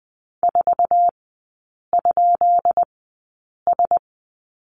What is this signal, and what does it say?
Morse code, heard as a keyed tone, about 19 wpm, spaced as for 8 wpm, 700 Hz. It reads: 4?S
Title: 133 Lesson 22e - Question Mark - Three Characters 20wpm